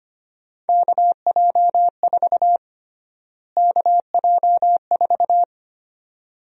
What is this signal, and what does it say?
Morse code, heard as a keyed tone, about 25 wpm, 700 Hz. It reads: KJ4 KJ4
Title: USA Call Sign Prefixes - Repeated 2x - 25wpm